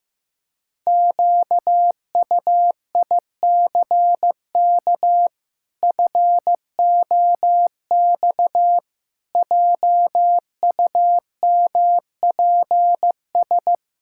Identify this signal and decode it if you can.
Morse code, heard as a keyed tone, about 15 wpm, 700 Hz. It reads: QUICK FOX JUMPS
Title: Pangram Challenge v2 - 15wpm